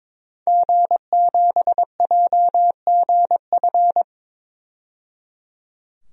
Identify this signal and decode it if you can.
Morse code, heard as a keyed tone, about 21 wpm, 700 Hz. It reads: G7JGF
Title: DXCC-100 Call Signs - 5 Characters - 22wpm